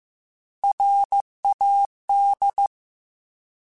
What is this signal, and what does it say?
Morse code, heard as a keyed tone, about 15 wpm, 800 Hz. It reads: RAD